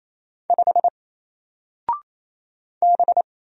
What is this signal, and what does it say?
Morse code, heard as a keyed tone, about 27 wpm, 700 Hz. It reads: 5 B
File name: Single-Letter-Number-28wpm